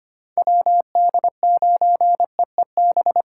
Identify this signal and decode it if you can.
Morse code, heard as a keyed tone, about 25 wpm, 700 Hz. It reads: WD9EEB
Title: USA Call Signs - 6 Characters - 2x Character Spacing - 25wpm